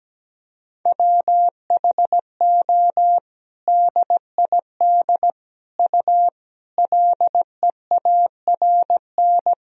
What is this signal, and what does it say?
Morse code, heard as a keyed tone, about 17 wpm, 700 Hz. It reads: WHO DID U LEARN
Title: Sentences from Top 300 Words and Abbreviations 17wpm